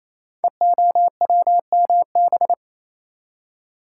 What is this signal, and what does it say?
Morse code, heard as a keyed tone, about 27 wpm, 700 Hz. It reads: EOWMB
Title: Code Groups - 5 Letter:Numbers - 28wpm